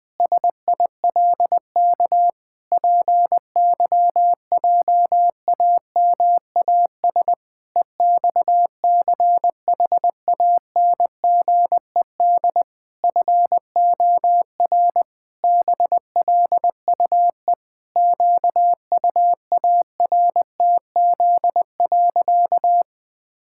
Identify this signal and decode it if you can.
Morse code, heard as a keyed tone, about 20 wpm, 700 Hz. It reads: SILK PYJAMAS EXCHANGED FOR BLUE QUARTZ.
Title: Pangram Challenge v2 - 20wpm